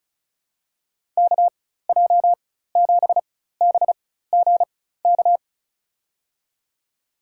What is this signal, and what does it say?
Morse code, heard as a keyed tone, about 34 wpm, spaced as for 16 wpm, 700 Hz. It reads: KJ7BGK
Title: USA Call Signs - 4x Character Spacing - 35wpm